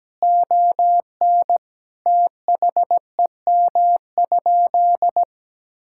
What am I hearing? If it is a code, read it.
Morse code, heard as a keyed tone, about 17 wpm, 700 Hz. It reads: ON THEM?
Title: Sentences from Top 1000 Words - Review Part 6 - 17wpm